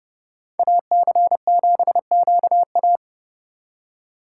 Morse code, heard as a keyed tone, about 38 wpm, 700 Hz. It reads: AC7QA